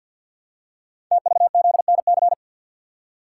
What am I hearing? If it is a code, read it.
Morse code, heard as a keyed tone, about 51 wpm, 700 Hz. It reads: TVZNC
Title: Code Groups - 5 Letter:Numbers - 50wpm